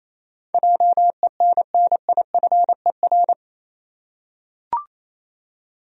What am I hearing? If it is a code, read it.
Morse code, heard as a keyed tone, about 28 wpm, 700 Hz. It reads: JENNIFER
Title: USA Common Names - 28wpm